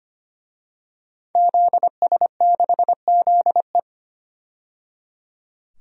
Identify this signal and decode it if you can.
Morse code, heard as a keyed tone, about 25 wpm, 700 Hz. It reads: ZS6ZE
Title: DXCC-100 Call Signs - 5 Characters - 25wpm